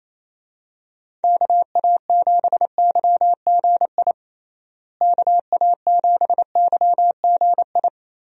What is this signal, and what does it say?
Morse code, heard as a keyed tone, about 28 wpm, 700 Hz. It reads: KA7YGI KA7YGI
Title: USA Call Signs - 6 Characters - Repeated 3x - 28wpm